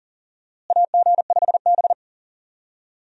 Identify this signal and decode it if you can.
Morse code, heard as a keyed tone, about 40 wpm, 700 Hz. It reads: AG5B
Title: USA Call Signs - 4 Characters - Repeated 2x - 40wpm